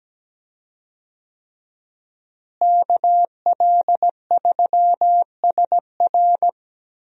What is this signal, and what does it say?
Morse code, heard as a keyed tone, about 17 wpm, 700 Hz. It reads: KL3SR
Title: USA Call Signs - 5 Characters - Repeated 3x - 17wpm